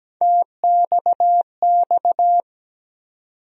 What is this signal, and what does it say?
Morse code, heard as a keyed tone, about 17 wpm, 700 Hz. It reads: TXX
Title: USA Call Sign Postfixes - 3 Letters 17wpm